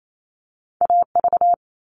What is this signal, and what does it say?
Morse code, heard as a keyed tone, about 28 wpm, 700 Hz. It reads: AV
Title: All Cut Numbers - 2-Digit 28wpm